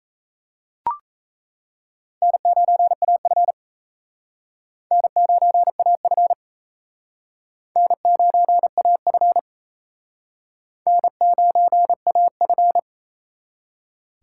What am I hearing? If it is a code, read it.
Morse code, heard as a keyed tone, about 35 wpm, 700 Hz. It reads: N9AF N9AF N9AF N9AF
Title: Speed Racing - USA 1x2 Call Signs 28wpm